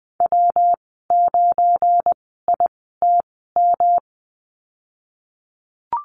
Morse code, heard as a keyed tone, about 20 wpm, spaced as for 15 wpm, 700 Hz. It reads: W9ITM
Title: 173 Lesson 28g - Number 9 - Call Signs 20wpm